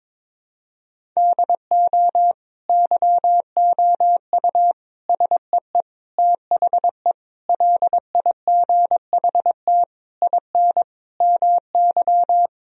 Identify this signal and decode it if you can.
Morse code, heard as a keyed tone, about 22 wpm, 700 Hz. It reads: DO YOU SEE THE LIGHT IN MY
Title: Sentences from Top 1000 Words - Review Part 1 - 22wpm